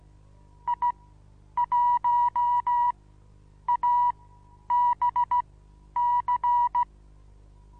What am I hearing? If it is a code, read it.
Morse code, heard as a keyed tone, about 16 wpm, spaced as for 9 wpm, 970 Hz. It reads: I1ABC
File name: Onlinelehrgang / morsen / DJ4UF Morsen25